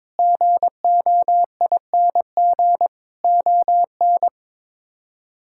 Morse code, heard as a keyed tone, about 22 wpm, 700 Hz. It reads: GOING ON
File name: practice / 22wpm / Sets of 8 Words - 22wpm